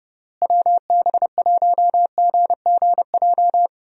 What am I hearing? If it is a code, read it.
Morse code, heard as a keyed tone, about 32 wpm, 700 Hz. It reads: WB1GGJ
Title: USA Call Signs - 6 Characters - Repeated 2x - 30wpm